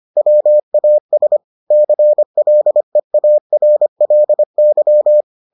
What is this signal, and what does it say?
Morse code, heard as a keyed tone, about 25 wpm, 600 Hz. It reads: WAS CLEARLY